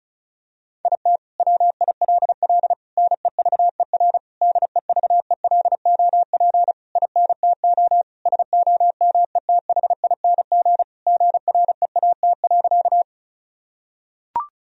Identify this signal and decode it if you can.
Morse code, heard as a keyed tone, about 36 wpm, 700 Hz. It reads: IT WILL NEVER DEVELOP INTO SOMETHING GREAT.
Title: Sentences from Top 1000 Words - Review Part 2 - 35wpm